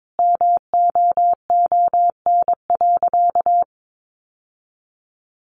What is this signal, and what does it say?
Morse code, heard as a keyed tone, about 22 wpm, 700 Hz. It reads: MOON.
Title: Sentences from Top 1300 Words 22wpm